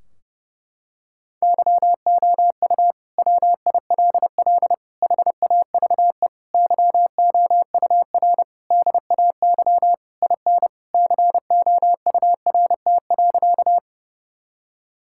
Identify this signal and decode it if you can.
Morse code, heard as a keyed tone, about 30 wpm, 700 Hz. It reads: YOU WILL HAVE YOUR DAY IN COURT.
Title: Sentences from Top 1600 Words 30wpm